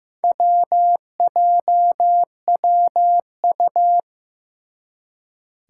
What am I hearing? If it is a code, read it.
Morse code, heard as a keyed tone, about 15 wpm, 700 Hz. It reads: WJWU
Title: Four Challenging Letter-Number Combinations 15wpm